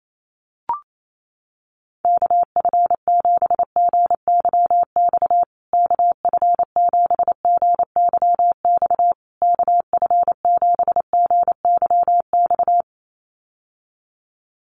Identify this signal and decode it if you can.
Morse code, heard as a keyed tone, about 28 wpm, 700 Hz. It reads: KF7GYX KF7GYX KF7GYX
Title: USA Call Signs Repeated 3x 28wpm